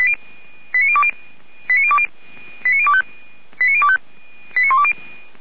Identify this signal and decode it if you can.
Five-tone selective call (ZVEI), heard as quick runs of five tones, tones of about 70 ms. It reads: 89020 89020 89025 89025 89120